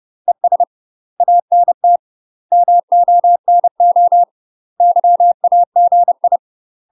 Morse code, heard as a keyed tone, about 30 wpm, 700 Hz. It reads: ES ANT MONO YAGI
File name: UK MASTER QSOs TRAINING / 30WPM / UK MASTER QSOs 0014